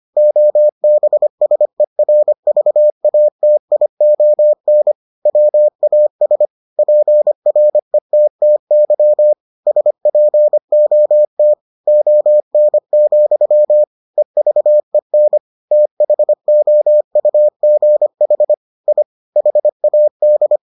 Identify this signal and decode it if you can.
Morse code, heard as a keyed tone, about 25 wpm, 600 Hz. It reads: OBSERVATION WAS PRETTY SPOT ON, EVEN THOUGH I HAD